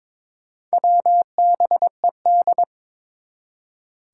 Morse code, heard as a keyed tone, about 22 wpm, 700 Hz. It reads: WBED
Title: Four Challenging Letter-Number Combinations 22wpm